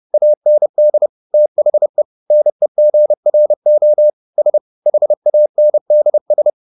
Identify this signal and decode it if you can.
Morse code, heard as a keyed tone, about 30 wpm, 600 Hz. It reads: AND THE NEGRO S HANDS